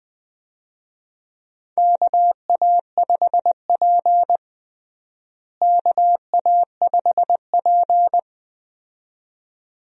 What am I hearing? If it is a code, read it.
Morse code, heard as a keyed tone, about 20 wpm, 700 Hz. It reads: KA5P KA5P
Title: USA Call Signs - 4 Characters - Repeated 2x - 20wpm